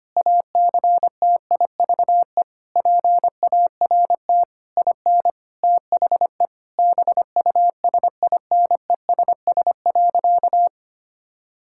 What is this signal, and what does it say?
Morse code, heard as a keyed tone, about 25 wpm, 700 Hz. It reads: ACTIVE PART IN THE BUSINESS.
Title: Sentences from Top 1000 Words - Review Part 6 - 25wpm